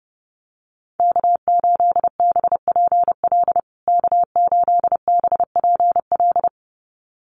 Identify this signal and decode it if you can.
Morse code, heard as a keyed tone, about 30 wpm, 700 Hz. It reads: K8BPL K8BPL
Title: USA Call Signs Repeated 2x 30wpm